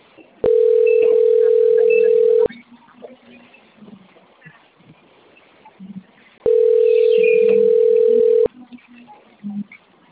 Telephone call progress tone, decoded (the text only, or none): ringback tone